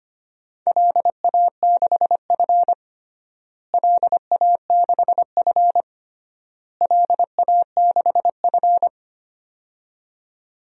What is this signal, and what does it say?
Morse code, heard as a keyed tone, about 25 wpm, 700 Hz. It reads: LA6F LA6F LA6F
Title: DXCC-100 Call Signs - 4 Characters - Repeated 3x - 25wpm